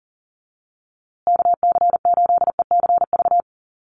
Morse code, heard as a keyed tone, about 40 wpm, 700 Hz. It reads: KC8ECV
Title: USA Call Signs Repeated 2x 40wpm